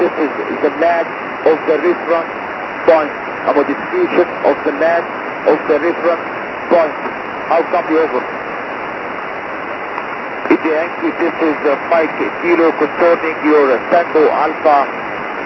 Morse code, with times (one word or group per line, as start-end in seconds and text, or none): none